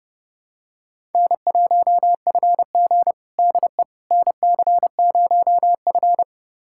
Morse code, heard as a keyed tone, about 32 wpm, 700 Hz. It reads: N1FG DE NC0F
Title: USA Callsign DE Callsign - 4 Characters - 30wpm